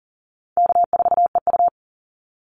Morse code, heard as a keyed tone, about 40 wpm, 700 Hz. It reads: K4EU
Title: Easy Call Signs Repeated 3x - 40wpm